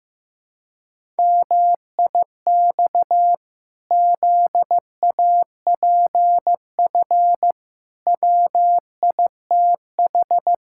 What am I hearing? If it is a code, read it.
Morse code, heard as a keyed tone, about 16 wpm, 700 Hz. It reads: MIX ZAPF WITH
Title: Pangram Challenge v2 - 15wpm